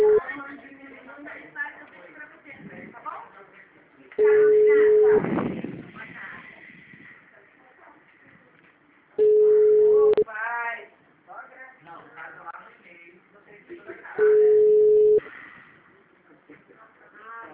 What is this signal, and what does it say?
Ringback tone (Europe, 425 Hz), with speech.
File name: out- -2005- - - .32038